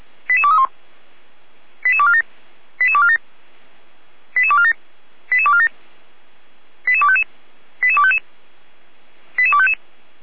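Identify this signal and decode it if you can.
Five-tone selective call (ZVEI), heard as quick runs of five tones, tones of about 70 ms. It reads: 88231 88247 88247 88247 88247 88255 88255 88255
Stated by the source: ZVEI five-tone selective call 88231 received on an SDR.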